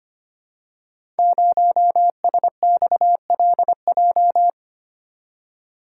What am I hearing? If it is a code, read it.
Morse code, heard as a keyed tone, about 25 wpm, 700 Hz. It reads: 0SXLJ
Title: Code Groups - 5 Letter:Numbers - 25wpm